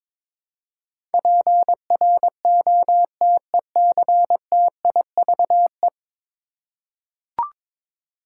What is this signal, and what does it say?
Morse code, heard as a keyed tone, about 23 wpm, 700 Hz. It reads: PROTECTIVE
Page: morse-code-ninja-practice-22wpm